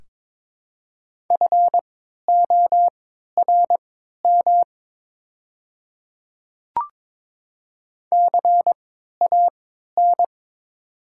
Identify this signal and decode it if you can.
Morse code, heard as a keyed tone, about 22 wpm, spaced as for 13 wpm, 700 Hz. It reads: FORM CAN
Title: 3x Character Spacing - Top 200 Words - 22wpm